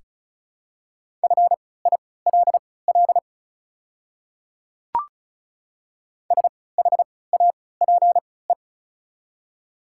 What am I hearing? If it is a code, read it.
Morse code, heard as a keyed tone, about 34 wpm, spaced as for 20 wpm, 690 Hz. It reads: FILL SHAPE
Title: 3x Character Spacing - Top 500 Words - 35wpm